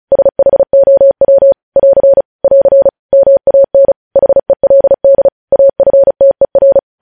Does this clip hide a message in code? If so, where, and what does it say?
Yes: Morse code from 0.1 to 6.8 s, heard as a keyed tone, about 36 wpm, 550 Hz. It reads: SHOW <AR> <AR> MAN HELD AFTER